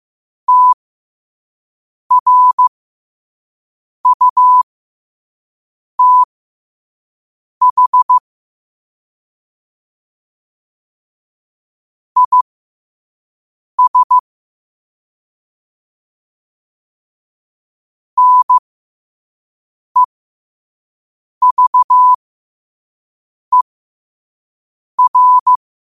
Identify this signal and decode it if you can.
Morse code, heard as a keyed tone, about 15 wpm, spaced as for 5 wpm, 1000 Hz. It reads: TRUTH IS NEVER
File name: qotdfiles / 5wpm